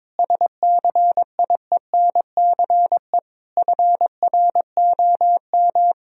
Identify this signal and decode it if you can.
Morse code, heard as a keyed tone, about 22 wpm, 700 Hz. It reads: SCIENCE FROM
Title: Sentences from Top 1000 Words - Review Part 2 - 22wpm